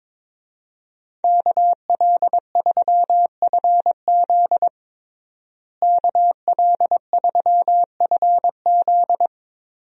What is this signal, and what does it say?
Morse code, heard as a keyed tone, about 22 wpm, 700 Hz. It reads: KL3FZ KL3FZ
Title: USA Call Signs - 5 Characters - Repeated 3x - 22wpm